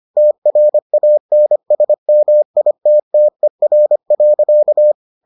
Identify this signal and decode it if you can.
Morse code, heard as a keyed tone, about 25 wpm, 600 Hz. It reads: TRANSMITTER.